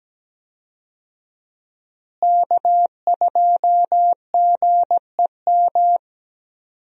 Morse code, heard as a keyed tone, about 17 wpm, 700 Hz. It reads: K2GEM
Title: USA Call Signs - 5 Characters - Repeated 2x - 17wpm